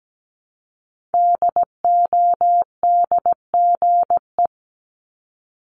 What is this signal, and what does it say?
Morse code, heard as a keyed tone, about 17 wpm, 700 Hz. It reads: DODGE
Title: Multi-letter Phonemes Repeated 3x 17wpm